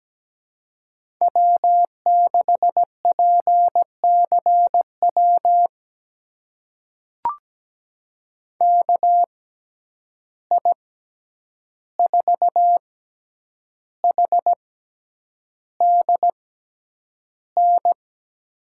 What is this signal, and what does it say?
Morse code, heard as a keyed tone, about 17 wpm, spaced as for 8 wpm, 700 Hz. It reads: W6PCW KI4HDN
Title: USA Call Signs - 6x Character Spacing - 17wpm